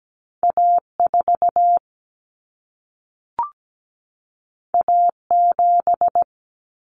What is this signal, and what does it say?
Morse code, heard as a keyed tone, about 17 wpm, 700 Hz. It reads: A4 A7
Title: Common Cut Numbers - 2-Digit 17wpm